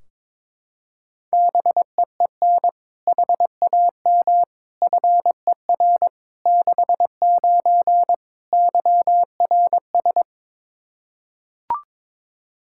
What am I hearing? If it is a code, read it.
Morse code, heard as a keyed tone, about 22 wpm, 700 Hz. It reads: BEEN HAM FER 69 YRS
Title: Practice QSO - Exchange 3 - More Info - 22wpm